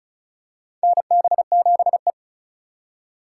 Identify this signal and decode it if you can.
Morse code, heard as a keyed tone, about 36 wpm, 700 Hz. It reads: NB7E
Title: USA Call Signs - 4 Characters - 8x Character Spacing - 35wpm